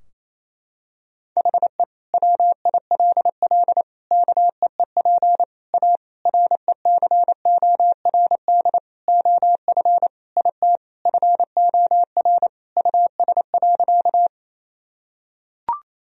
Morse code, heard as a keyed tone, about 28 wpm, 700 Hz. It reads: HE WILL KEEP A RECORD OF IT FOR US.
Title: Sentences from Top 1000 Words - Review Part 3 - 28wpm